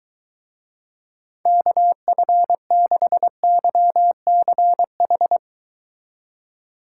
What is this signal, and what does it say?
Morse code, heard as a keyed tone, about 23 wpm, 700 Hz. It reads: KF6YCH